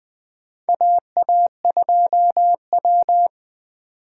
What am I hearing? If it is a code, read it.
Morse code, heard as a keyed tone, about 20 wpm, 700 Hz. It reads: AA2W
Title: USA Call Signs - 4 Characters - 4x Character Spacing - 20wpm